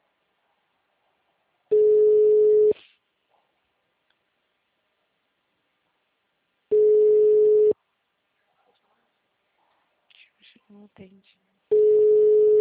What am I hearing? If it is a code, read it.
Ringback tone (Europe, 425 Hz), with speech.
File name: out- -2003- - - .5